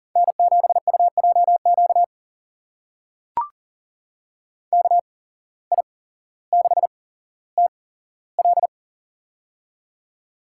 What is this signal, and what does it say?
Morse code, heard as a keyed tone, about 40 wpm, spaced as for 17 wpm, 700 Hz. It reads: N7UJQ KI6TL
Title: USA Call Signs - 5 Characters - 8x Character Spacing - 40wpm